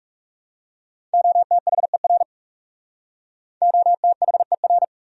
Morse code, heard as a keyed tone, about 39 wpm, 700 Hz. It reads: OTHER OTHER